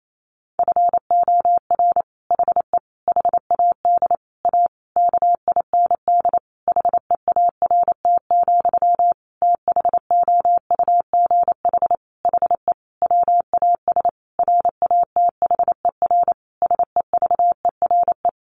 Morse code, heard as a keyed tone, about 28 wpm, 700 Hz. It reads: FOR HE HAD A KIND HEART, THOUGH HE WAS RATHER SEVERE
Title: Sentences from Top 1400 Words 28wpm